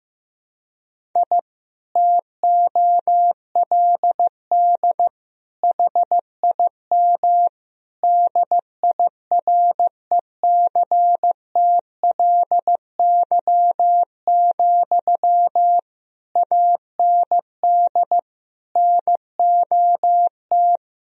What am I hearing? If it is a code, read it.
Morse code, heard as a keyed tone, about 15 wpm, 700 Hz. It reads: I TOLD HIM DIRECTLY, AND NOT
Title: Sentences from Top 1000 Words - Review Part 1 - 15wpm